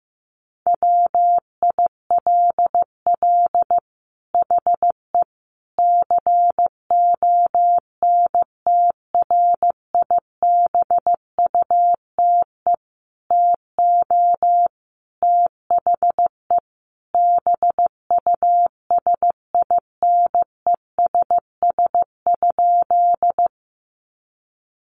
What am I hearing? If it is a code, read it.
Morse code, heard as a keyed tone, about 15 wpm, 700 Hz. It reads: WILL HE CONTRIBUTE TO THE BUSINESS?
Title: Sentences from Top 1400 Words 15wpm